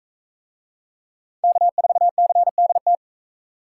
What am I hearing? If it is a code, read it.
Morse code, heard as a keyed tone, about 42 wpm, 700 Hz. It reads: K4CDT